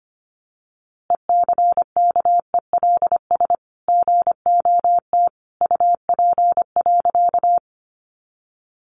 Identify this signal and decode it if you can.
Morse code, heard as a keyed tone, about 25 wpm, 700 Hz. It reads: ECKELS GOT UP.
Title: A Sound Of Thunder 25wpm